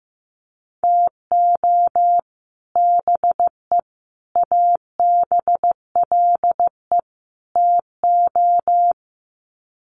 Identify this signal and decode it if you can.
Morse code, heard as a keyed tone, about 15 wpm, 700 Hz. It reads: TO BE ABLE TO
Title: Sets of 4 Words 15wpm